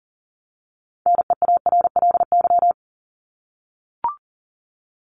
Morse code, heard as a keyed tone, about 40 wpm, 700 Hz. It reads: NEARLY
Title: LY Suffix - Top 100 Words with Base Words 40wpm